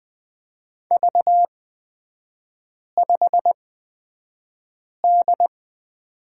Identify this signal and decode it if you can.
Morse code, heard as a keyed tone, about 20 wpm, spaced as for 5 wpm, 700 Hz. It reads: V5D
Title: 250 Lesson 40e - BK - Three Characters 20wpm